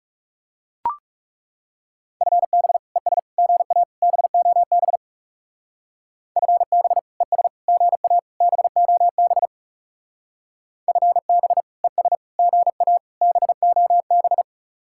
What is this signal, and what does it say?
Morse code, heard as a keyed tone, about 37 wpm, 700 Hz. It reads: FB ES GA BOB FB ES GA BOB FB ES GA BOB
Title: Speed-Racing Common QSO Phrases - 30wpm